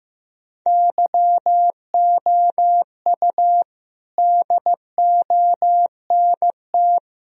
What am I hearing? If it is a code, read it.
Morse code, heard as a keyed tone, about 15 wpm, 700 Hz. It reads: YOU DONT
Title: Sentences from Top 1000 Words - Review Part 1 - 15wpm